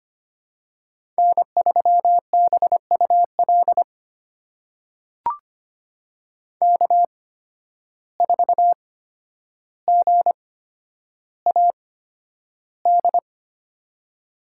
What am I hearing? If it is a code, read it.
Morse code, heard as a keyed tone, about 25 wpm, spaced as for 11 wpm, 700 Hz. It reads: N3BUL K4GAD
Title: USA Call Signs - 5 Characters - 8x Character Spacing - 25wpm